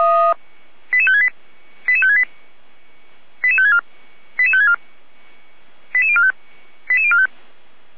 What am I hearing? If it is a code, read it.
Five-tone selective call (ZVEI), heard as quick runs of five tones, tones of about 70 ms. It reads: 88568 88568 88564 88564 88045 88045